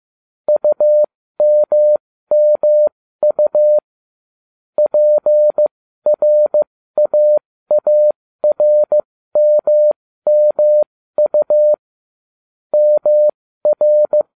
Morse code, heard as a keyed tone, about 15 wpm, spaced as for 13 wpm, 600 Hz. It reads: UMMU PRAARMMU MR